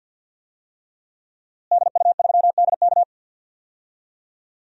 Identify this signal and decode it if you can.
Morse code, heard as a keyed tone, about 46 wpm, 700 Hz. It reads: DU3DK
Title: DXCC-100 Call Signs - 5 Characters - Repeated 3x - 50wpm